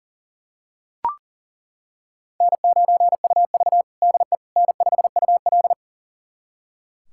Morse code, heard as a keyed tone, about 40 wpm, 700 Hz. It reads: N9UV DE N5UL
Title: USA Callsign DE Callsign - 4 Characters - 40wpm